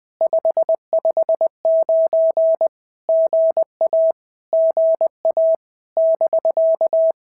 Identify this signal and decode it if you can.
Morse code, heard as a keyed tone, about 20 wpm, 650 Hz. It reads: 559 GA GA <BK>
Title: POTA QSO Practice - 20wpm